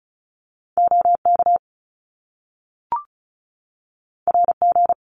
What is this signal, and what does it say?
Morse code, heard as a keyed tone, about 36 wpm, 700 Hz. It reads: OK RG
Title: Top Two Letter Combinations in Words 35wpm